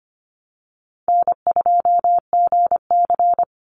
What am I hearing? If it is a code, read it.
Morse code, heard as a keyed tone, about 25 wpm, 700 Hz. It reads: N2GC
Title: Easy Call Signs - 25wpm